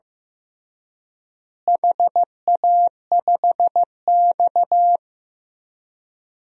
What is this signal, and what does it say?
Morse code, heard as a keyed tone, about 15 wpm, 700 Hz. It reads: HA5X
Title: DXCC-100 Call Signs - 4 Characters - Repeated 2x - 15wpm